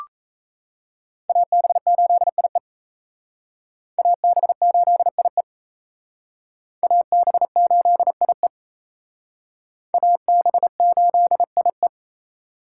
Morse code, heard as a keyed tone, about 34 wpm, 700 Hz. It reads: AB8IE AB8IE AB8IE AB8IE
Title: Speed Racing - USA 2x2 Call Signs 28wpm